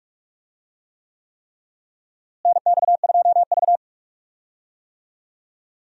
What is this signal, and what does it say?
Morse code, heard as a keyed tone, about 47 wpm, 700 Hz. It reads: NX2V
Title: USA Call Signs - 4 Characters - Repeated 3x - 45wpm